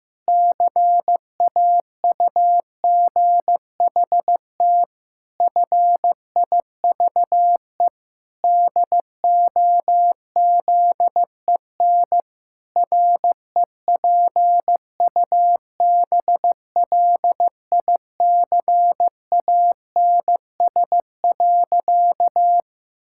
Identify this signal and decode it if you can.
Morse code, heard as a keyed tone, about 14 wpm, 700 Hz. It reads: CAUGHT FIVE DOZEN REPUBLICANS.
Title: Pangram Challenge v2 - 15wpm